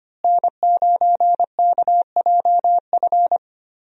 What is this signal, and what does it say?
Morse code, heard as a keyed tone, about 25 wpm, 700 Hz. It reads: N9KJF
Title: USA Call Signs - 5 Characters - Repeated 3x - 25wpm